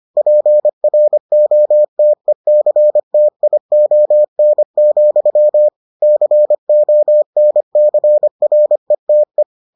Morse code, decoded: PROTECTION, CONCRETE

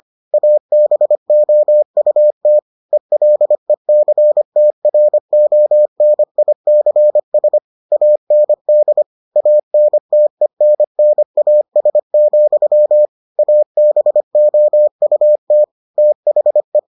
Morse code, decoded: ABOUT ELECTRONICS AND ANTENNAS, ABOUT THE

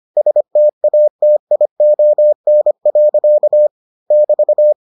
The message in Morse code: STATION. <BT>

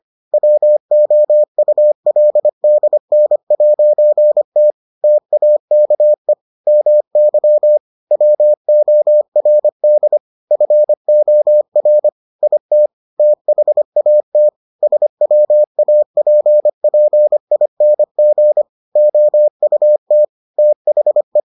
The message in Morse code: WOULDN'T TAKE MY WORD FOR IT THAT SWAPPING OUT THE

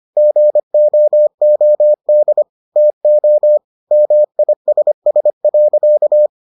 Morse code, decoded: GOOD TO MISS.

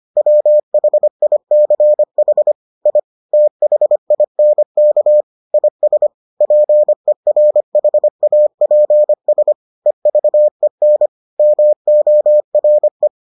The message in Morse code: WHICH I THINK IS PERHAPS EVEN MORE

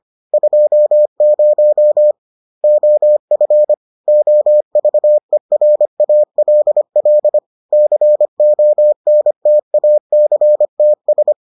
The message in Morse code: 20 OF OVERALL CONTACTS

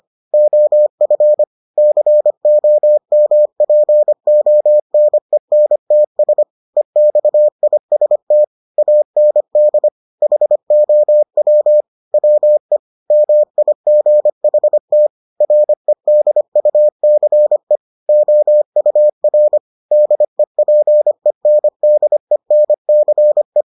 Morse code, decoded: OF COMPONENTS EXIST AND HOW WE MIGHT REDUCE OUR DEPENDENCE

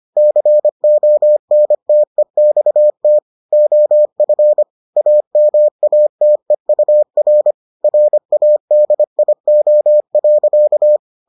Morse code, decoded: CONTEXT OF AMATEUR RADIO.

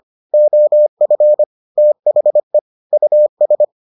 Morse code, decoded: OF THE US

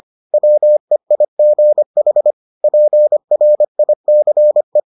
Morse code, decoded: WEIGH PRICE

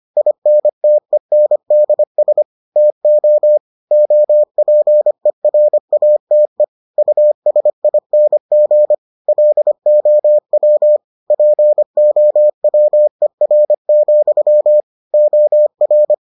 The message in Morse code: INTENDS TO OPERATE USING LOW POWER, OR

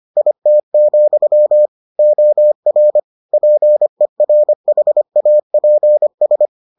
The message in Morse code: IT, OR PERHAPS

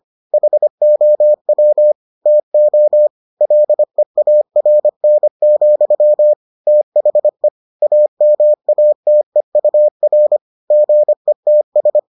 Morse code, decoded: HOW TO LEARN, THE AMATEUR GETS

